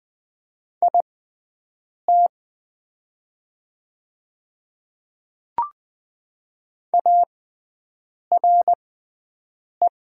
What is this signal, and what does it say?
Morse code, heard as a keyed tone, about 20 wpm, spaced as for 7 wpm, 700 Hz. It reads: IT ARE
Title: 6x Character Spacing - Top 100 Words - 20wpm